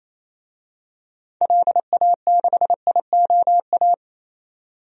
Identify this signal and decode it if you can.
Morse code, heard as a keyed tone, about 28 wpm, 700 Hz. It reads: LA6IOA
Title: DXCC-100 Call Signs - 6 Characters - 28wpm